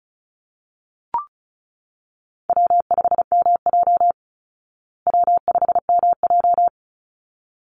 Morse code, heard as a keyed tone, about 34 wpm, 700 Hz. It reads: W5MJ W5MJ
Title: Easy Call Signs Repeated 2x - 35wpm